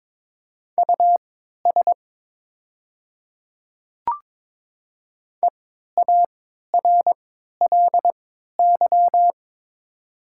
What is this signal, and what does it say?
Morse code, heard as a keyed tone, about 22 wpm, spaced as for 12 wpm, 700 Hz. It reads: US EARLY
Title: 3x Character Spacing - Top 500 Words - 22wpm